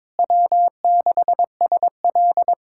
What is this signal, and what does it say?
Morse code, heard as a keyed tone, about 22 wpm, 700 Hz. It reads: W6SL
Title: USA Callsign DE Callsign - 4 Characters - 22wpm